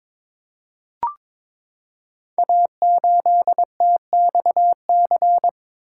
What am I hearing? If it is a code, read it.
Morse code, heard as a keyed tone, about 22 wpm, 700 Hz. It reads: A8TXC
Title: Code Groups - 5 Letter:Numbers - 22wpm